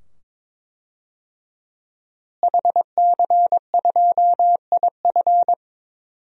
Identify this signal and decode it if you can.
Morse code, heard as a keyed tone, about 22 wpm, 700 Hz. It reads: HC2IF
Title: DXCC-100 Call Signs - 5 Characters - Repeated 2x - 22wpm